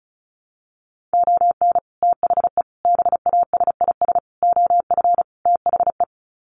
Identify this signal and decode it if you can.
Morse code, heard as a keyed tone, about 34 wpm, 700 Hz. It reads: ON THE BASIS OF THE
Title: Sets of 5 Words 35wpm